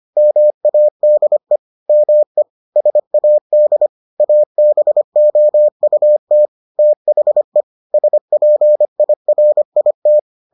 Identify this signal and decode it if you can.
Morse code, heard as a keyed tone, about 25 wpm, 600 Hz. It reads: MADE ME SAD ABOUT THE SPIRIT